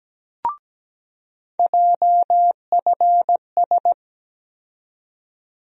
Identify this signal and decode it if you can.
Morse code, heard as a keyed tone, about 17 wpm, 700 Hz. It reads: JFS
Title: USA Call Sign Postfixes - 3 Letters 17wpm